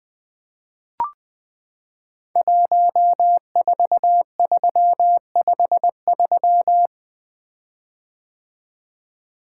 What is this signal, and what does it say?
Morse code, heard as a keyed tone, about 20 wpm, 700 Hz. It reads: 14353
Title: Code Groups - 5 Letter:Numbers - 20wpm